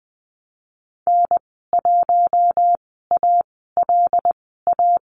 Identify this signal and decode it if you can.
Morse code, heard as a keyed tone, about 20 wpm, spaced as for 15 wpm, 700 Hz. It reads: N1ALA
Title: 122 Lesson 20g - Number 3 - Call Signs 20wpm